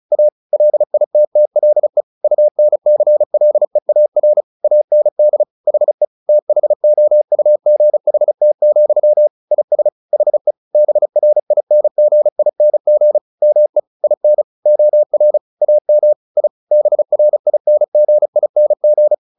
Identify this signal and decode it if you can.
Morse code, heard as a keyed tone, about 34 wpm, 600 Hz. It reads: A LITTLE UNCLEAR AND HE THOUGHT, IS HE BRINGING ME IN OR AM I BRINGING
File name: the old man and the sea 2 35wpm